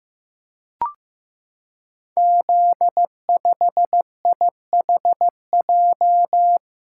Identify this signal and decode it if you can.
Morse code, heard as a keyed tone, about 16 wpm, 700 Hz. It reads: Z5IHJ
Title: Code Groups - 5 Letter:Numbers - 15wpm